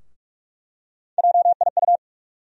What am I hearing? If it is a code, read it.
Morse code, heard as a keyed tone, about 46 wpm, 700 Hz. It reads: JIU
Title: USA Call Sign Postfixes - 3 Letters 45wpm